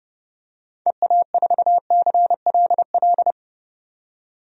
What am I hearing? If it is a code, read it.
Morse code, heard as a keyed tone, about 30 wpm, 700 Hz. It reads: EA4CLL
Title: DXCC-100 Call Signs - 6 Characters - 30wpm